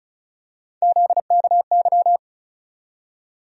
Morse code, heard as a keyed tone, about 34 wpm, 700 Hz. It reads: ZKY
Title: USA Call Sign Postfixes - 3 Letters 35wpm